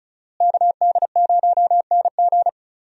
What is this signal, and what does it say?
Morse code, heard as a keyed tone, about 36 wpm, 700 Hz. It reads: KD0NG
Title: USA Call Signs - 5 Characters - Repeated 3x - 35wpm